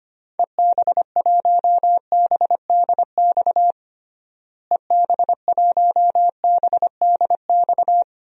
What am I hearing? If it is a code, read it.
Morse code, heard as a keyed tone, about 25 wpm, 700 Hz. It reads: EB1BDX EB1BDX
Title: DXCC-100 Call Signs - 6 Characters - Repeated 2x - 25wpm